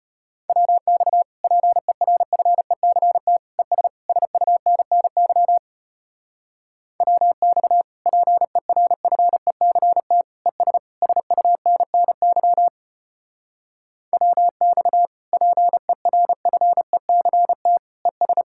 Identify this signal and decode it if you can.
Morse code, heard as a keyed tone, about 33 wpm, 700 Hz. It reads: WX PERFECT ES SUNNY WX PERFECT ES SUNNY WX PERFECT ES